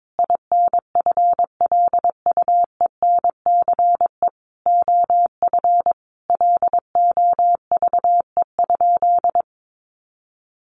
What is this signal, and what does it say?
Morse code, heard as a keyed tone, about 22 wpm, 700 Hz. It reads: INFLUENCE OF LOVE?
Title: Sentences from Top 1100 Words 22wpm